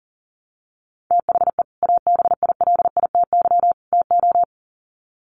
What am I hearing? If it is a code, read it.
Morse code, heard as a keyed tone, about 40 wpm, 700 Hz. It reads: THE ABILITY TO
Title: Sets of 3 Words 40wpm